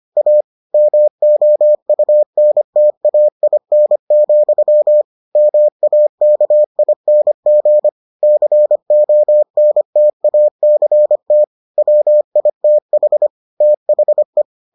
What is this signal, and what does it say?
Morse code, heard as a keyed tone, about 25 wpm, 600 Hz. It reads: A MOUNTAIN, MAKING CONTACT WITH THE